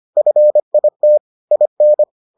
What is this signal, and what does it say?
Morse code, heard as a keyed tone, about 25 wpm, 600 Hz. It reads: FIT IN